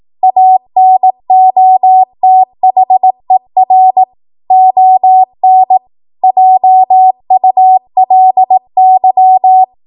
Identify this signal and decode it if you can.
Morse code, heard as a keyed tone, about 18 wpm, 750 Hz. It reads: ANOTHER ON JULY
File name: Archive / 18 WPM / 18WPM